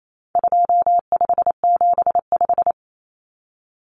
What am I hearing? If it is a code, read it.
Morse code, heard as a keyed tone, about 28 wpm, 700 Hz. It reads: 2575
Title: Numbers - 4-Digit 28wpm